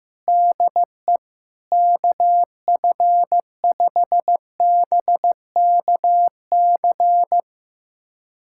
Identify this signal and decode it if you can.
Morse code, heard as a keyed tone, about 15 wpm, 700 Hz. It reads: DE KF5BKC
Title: USA Callsign DE Callsign - 6 Characters - 15wpm